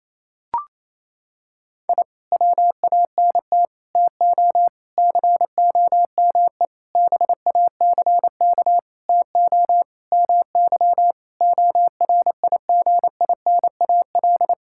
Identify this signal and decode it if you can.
Morse code, heard as a keyed tone, about 28 wpm, 700 Hz. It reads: I WANT TO COME BACK TO MY ORIGINAL
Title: Sentences from Top 1000 Words - Review Part 6 - 28wpm